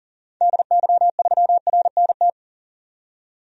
Morse code, heard as a keyed tone, about 40 wpm, 700 Hz. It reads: DY3RNT
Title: DXCC-100 Call Signs - 6 Characters - 40wpm